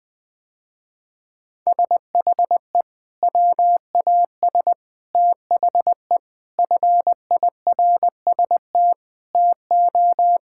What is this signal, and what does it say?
Morse code, heard as a keyed tone, about 20 wpm, 700 Hz. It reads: SHE WAS THE FIRST TO
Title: Sentences with 3-Word Phrases - Spoken Beginning - 20wpm